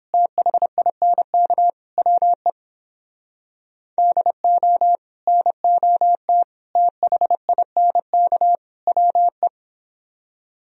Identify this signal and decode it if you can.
Morse code, heard as a keyed tone, about 29 wpm, 700 Hz. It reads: THINK WE DO NOT THINK WE